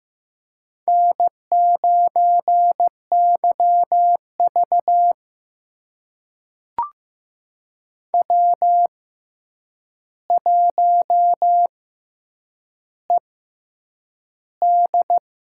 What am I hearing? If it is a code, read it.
Morse code, heard as a keyed tone, about 15 wpm, spaced as for 8 wpm, 700 Hz. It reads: N9YV W1ED
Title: USA Call Signs - 4 Characters - 6x Character Spacing - 15wpm